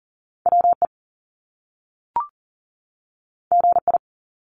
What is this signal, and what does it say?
Morse code, heard as a keyed tone, about 38 wpm, 700 Hz. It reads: WE GI
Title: Top Two Letter Combinations in Words 40wpm